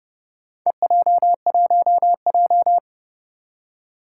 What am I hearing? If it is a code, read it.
Morse code, heard as a keyed tone, about 30 wpm, 700 Hz. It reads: EJ1J
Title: Four Challenging Letter-Number Combinations 30wpm